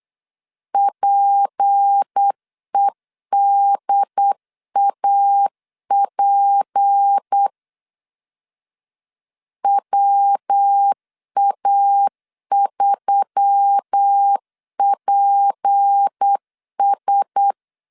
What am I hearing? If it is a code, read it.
Morse code, heard as a keyed tone, about 8 wpm, 800 Hz. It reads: PEDAP WA3PS
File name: LEZIONI / 6 PKX0 / 6 P 8wpm 3025